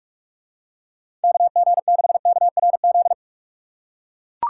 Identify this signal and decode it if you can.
Morse code, heard as a keyed tone, about 41 wpm, 700 Hz. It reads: KG6KRZ
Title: USA Call Signs - 6 Characters - 6x Character Spacing - 45wpm